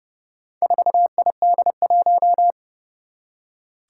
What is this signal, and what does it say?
Morse code, heard as a keyed tone, about 30 wpm, 700 Hz. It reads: 4ID1
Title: Four Challenging Letter-Number Combinations 30wpm